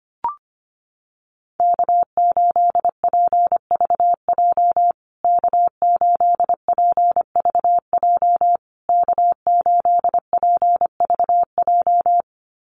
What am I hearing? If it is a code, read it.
Morse code, heard as a keyed tone, about 25 wpm, 700 Hz. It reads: K8PVJ K8PVJ K8PVJ
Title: USA Call Signs Repeated 3x 25wpm